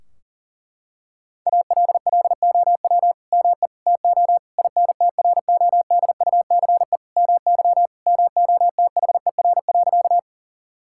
Morse code, heard as a keyed tone, about 40 wpm, 700 Hz. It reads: ALLOW ME TO INTRODUCE MY MOTHER.
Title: Sentences from Top 1600 Words 40wpm